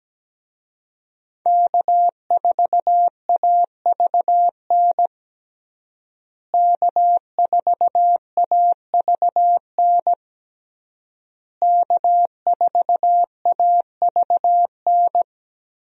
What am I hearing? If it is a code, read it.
Morse code, heard as a keyed tone, about 17 wpm, 700 Hz. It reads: K4AVN K4AVN K4AVN
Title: USA Call Signs - 5 Characters - Repeated 3x - 17wpm